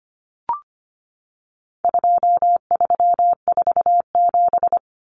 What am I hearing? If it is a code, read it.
Morse code, heard as a keyed tone, about 25 wpm, 700 Hz. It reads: 2347
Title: Numbers - 4-Digit 25wpm